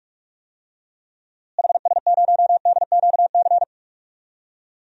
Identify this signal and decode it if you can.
Morse code, heard as a keyed tone, about 43 wpm, 700 Hz. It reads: HS0DQC
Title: DXCC-100 Call Signs - 6 Characters - Repeated 2x - 45wpm